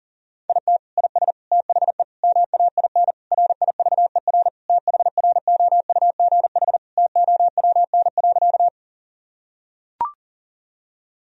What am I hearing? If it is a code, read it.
Morse code, heard as a keyed tone, about 37 wpm, 700 Hz. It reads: IT IS THE MAIN RIVER THROUGH TOWN.
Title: Sentences from Top 1000 Words - Review Part 2 - 40wpm